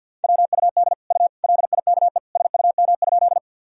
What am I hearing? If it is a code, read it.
Morse code, heard as a keyed tone, about 52 wpm, 700 Hz. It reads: WUD U LIKE SUM?
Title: Sentences from Top 100 Words and Abbreviations 50wpm